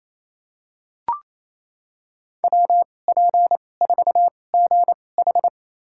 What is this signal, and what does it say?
Morse code, heard as a keyed tone, about 28 wpm, spaced as for 20 wpm, 700 Hz. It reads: WP4GH
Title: USA Call Signs - 5 Characters - 2x Character Spacing - 28wpm